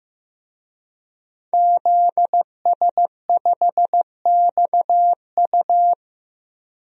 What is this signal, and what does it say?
Morse code, heard as a keyed tone, about 15 wpm, 700 Hz. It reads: ZS5XU
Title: DXCC-100 Call Signs - 5 Characters - Repeated 3x - 15wpm